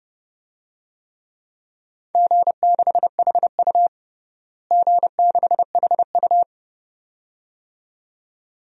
Morse code, heard as a keyed tone, about 32 wpm, 700 Hz. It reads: G6HU G6HU
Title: DXCC-100 Call Signs - 4 Characters - Repeated 2x - 30wpm